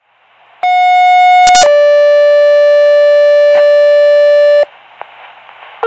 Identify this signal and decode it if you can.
Two-tone page, heard as a one-second tone, then a three-second tone, about 730 then 580 Hz. It is heard from 0.6 to 4.6 s.